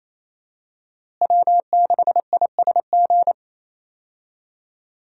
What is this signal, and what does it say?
Morse code, heard as a keyed tone, about 28 wpm, 700 Hz. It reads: W6ISG
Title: USA Call Signs - 5 Characters - Repeated 2x - 28wpm